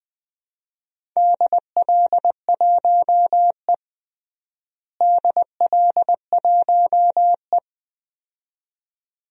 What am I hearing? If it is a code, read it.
Morse code, heard as a keyed tone, about 20 wpm, 700 Hz. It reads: DL1E DL1E
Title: DXCC-100 Call Signs - 4 Characters - Repeated 2x - 20wpm